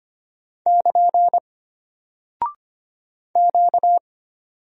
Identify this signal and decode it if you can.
Morse code, heard as a keyed tone, about 25 wpm, 700 Hz. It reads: <KN> Q
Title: Single Character (Letter:Number:Punctuation:Prosign) Alt 25wpm